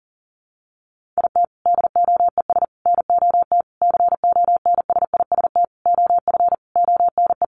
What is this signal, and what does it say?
Morse code, heard as a keyed tone, about 40 wpm, 700 Hz. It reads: IT DOES NOT CONSIST OF ONE
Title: Sentences from Top 1400 Words 40wpm